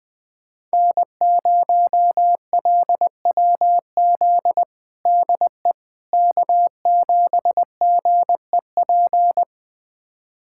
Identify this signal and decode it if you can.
Morse code, heard as a keyed tone, about 21 wpm, 700 Hz. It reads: N0LWZ DE K7GEP
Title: USA Callsign DE Callsign - 5 Characters - 20wpm